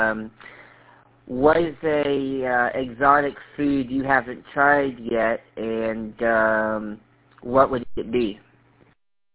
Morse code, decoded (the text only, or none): none